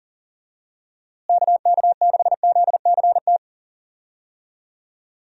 Morse code, heard as a keyed tone, about 40 wpm, 700 Hz. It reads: KK6ZCT
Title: USA Call Signs - 6 Characters - Repeated 3x - 40wpm